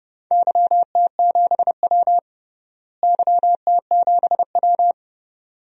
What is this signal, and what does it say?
Morse code, heard as a keyed tone, about 32 wpm, 700 Hz. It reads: YT7W YT7W
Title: DXCC-100 Call Signs - 4 Characters - Repeated 2x - 30wpm